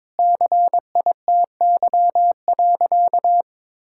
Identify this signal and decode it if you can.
Morse code, heard as a keyed tone, about 22 wpm, 700 Hz. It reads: CITY.
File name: practice / 22wpm / Sentences with 3-Word Phrases - 22wpm